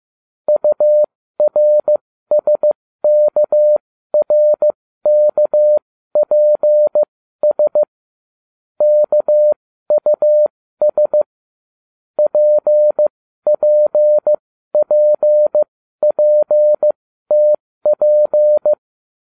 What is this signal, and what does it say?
Morse code, heard as a keyed tone, about 15 wpm, spaced as for 12 wpm, 600 Hz. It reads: URSKRKPS KUS PPPPTP